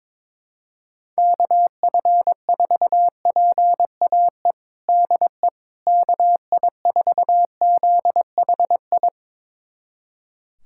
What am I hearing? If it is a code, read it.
Morse code, heard as a keyed tone, about 22 wpm, 700 Hz. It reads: KF4PAE DE KI4ZHI
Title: USA Callsign DE Callsign - 6 Characters - 22wpm